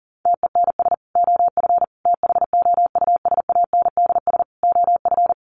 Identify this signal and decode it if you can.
Morse code, heard as a keyed tone, about 40 wpm, 700 Hz. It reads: TENS OF THOUSANDS OF
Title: Sets of 4 Words 40wpm